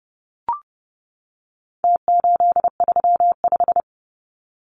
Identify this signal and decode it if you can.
Morse code, heard as a keyed tone, about 30 wpm, 700 Hz. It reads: T835
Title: Numbers - Contest Style - 4-Digit 30wpm